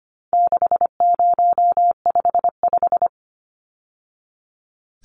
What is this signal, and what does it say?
Morse code, heard as a keyed tone, about 25 wpm, 700 Hz. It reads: 6055
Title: Numbers - 4-Digit 25wpm